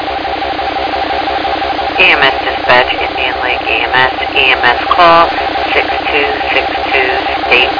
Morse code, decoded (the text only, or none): none